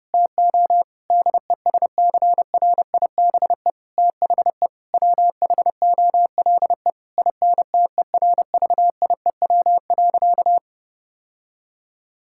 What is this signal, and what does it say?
Morse code, heard as a keyed tone, about 30 wpm, 700 Hz. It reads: TO DESCRIBE THE WHOLE INTERVIEW.
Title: Sentences from Top 1600 Words 30wpm